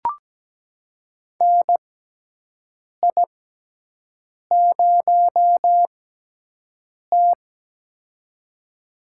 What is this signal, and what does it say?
Morse code, heard as a keyed tone, about 17 wpm, spaced as for 6 wpm, 700 Hz. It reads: NI0T
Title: USA Call Signs - 6x Character Spacing - 17wpm